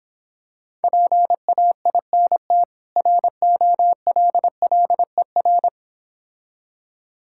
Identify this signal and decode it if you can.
Morse code, heard as a keyed tone, about 26 wpm, 700 Hz. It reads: PAINT ROLLER